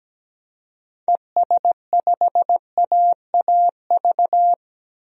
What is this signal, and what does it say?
Morse code, heard as a keyed tone, about 17 wpm, 700 Hz. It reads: ES5AAV
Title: DXCC-100 Call Signs - 6 Characters - Repeated 3x - 17wpm